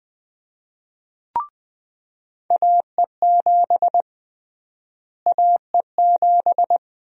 Morse code, heard as a keyed tone, about 21 wpm, 700 Hz. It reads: AE7 AE7
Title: USA Call Sign Prefixes - Repeated 2x - 20wpm